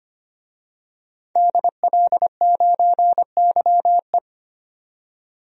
Morse code, heard as a keyed tone, about 25 wpm, 700 Hz. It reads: DL9YE
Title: DXCC-100 Call Signs - 5 Characters - 25wpm